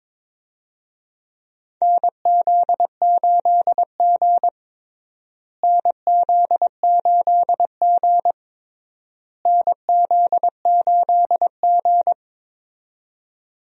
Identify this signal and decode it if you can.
Morse code, heard as a keyed tone, about 22 wpm, 700 Hz. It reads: NZ8G NZ8G NZ8G
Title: USA Call Signs - 4 Characters - Repeated 3x - 22wpm